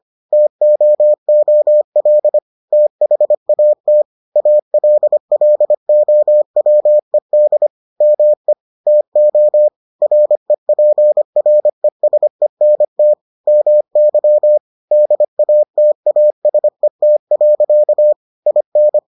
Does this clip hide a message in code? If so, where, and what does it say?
Yes: Morse code from 0.3 to 19.0 s, heard as a keyed tone, about 25 wpm, 600 Hz. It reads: TOOL THAT ALLOWED ME TO REPRESENT MY DATASET. IN